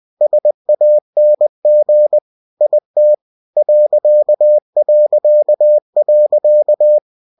Morse code, heard as a keyed tone, about 20 wpm, 600 Hz. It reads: SANG IT ...